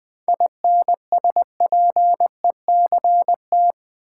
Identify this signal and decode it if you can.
Morse code, heard as a keyed tone, about 20 wpm, 700 Hz. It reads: INSPECT
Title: 7 Letter Words - First 3 Letters Given - 20wpm